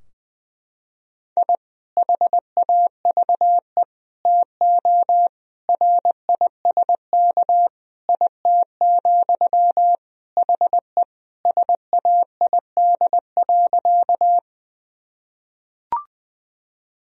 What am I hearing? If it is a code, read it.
Morse code, heard as a keyed tone, about 21 wpm, 700 Hz. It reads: I HAVE TO RISK IT, HE SAID.
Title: Sentences from Top 1000 Words - Review Part 4 - 20wpm